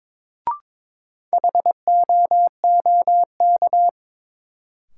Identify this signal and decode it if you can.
Morse code, heard as a keyed tone, about 22 wpm, 700 Hz. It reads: HOOK
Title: Sentences from Top 1600 Words - New Word Drill Alt - 22wpm